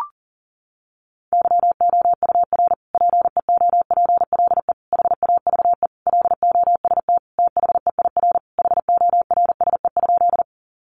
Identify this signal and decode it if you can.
Morse code, heard as a keyed tone, about 40 wpm, 700 Hz. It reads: YOUR PEOPLE HAVE LOST THEIR HORSE?
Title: Sentences from Top 1400 Words 40wpm